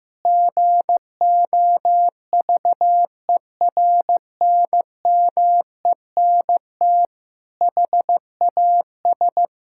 Morse code, heard as a keyed tone, about 14 wpm, 700 Hz. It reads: GOVERNMENT HAS
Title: Sentences from Top 1000 Words - Review Part 5 - 15wpm